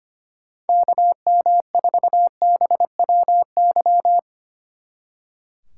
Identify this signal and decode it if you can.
Morse code, heard as a keyed tone, about 30 wpm, 700 Hz. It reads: KM4BWY
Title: Speed Racing - USA 2x3 Call Signs 25wpm